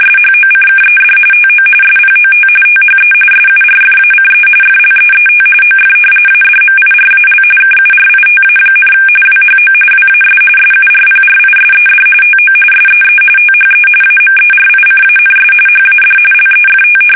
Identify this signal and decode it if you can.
RTTY, about 45.45 baud. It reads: SHOCKED TO SPOT THE SURFACE FLEET OF ADMIRAL KURITA, WHO WAS SURPRISED TO SEE THE AMERICANS HAD LEFT
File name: fsk850 / battleshipnj